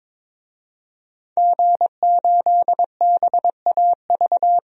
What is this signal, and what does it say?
Morse code, heard as a keyed tone, about 22 wpm, 700 Hz. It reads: G8BAV
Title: DXCC-100 Call Signs - 5 Characters - 22wpm